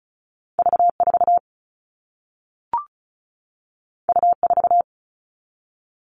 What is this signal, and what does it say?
Morse code, heard as a keyed tone, about 36 wpm, 700 Hz. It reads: V4 U4
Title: All Cut Numbers - 2-Digit 35wpm